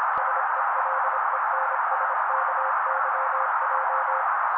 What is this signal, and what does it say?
Morse code, heard as a keyed tone, about 26 wpm, 530 Hz. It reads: SELENSKYJ
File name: daily / 25wpm noise